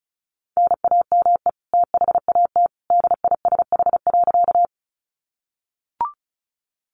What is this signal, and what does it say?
Morse code, heard as a keyed tone, about 34 wpm, 700 Hz. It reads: NAME THAT DISH.
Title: Sentences from Top 1300 Words 35wpm